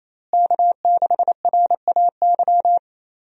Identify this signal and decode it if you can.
Morse code, heard as a keyed tone, about 28 wpm, 700 Hz. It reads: K6RAY
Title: USA Call Signs - 5 Characters - Repeated 2x - 28wpm